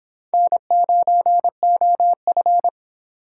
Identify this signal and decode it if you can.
Morse code, heard as a keyed tone, about 24 wpm, 700 Hz. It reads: N9OF